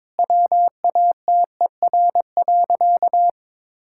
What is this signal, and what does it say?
Morse code, heard as a keyed tone, about 22 wpm, 700 Hz. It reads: WATER.
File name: practice / 22wpm / Sentences with 4-Word Phrases - 22wpm